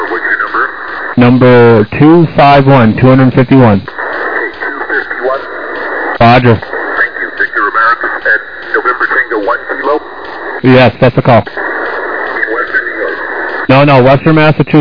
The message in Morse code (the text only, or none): none